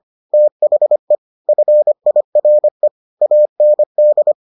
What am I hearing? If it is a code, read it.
Morse code, heard as a keyed tone, about 25 wpm, 600 Hz. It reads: THE FIRE AND